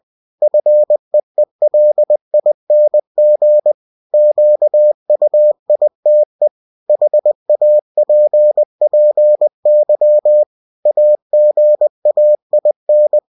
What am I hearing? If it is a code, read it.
Morse code, heard as a keyed tone, about 20 wpm, 600 Hz. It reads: FEELING QUITE HAPPY AGAIN
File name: cw-books / winnie-the-pooh / winnie-20wpm-page-0034